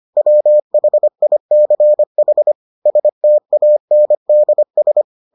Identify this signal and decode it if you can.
Morse code, heard as a keyed tone, about 25 wpm, 600 Hz. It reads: WHICH STANDS